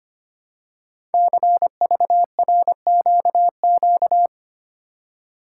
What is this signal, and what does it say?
Morse code, heard as a keyed tone, about 25 wpm, 700 Hz. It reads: CVRQQ
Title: Code Groups - 5 Letters - 25wpm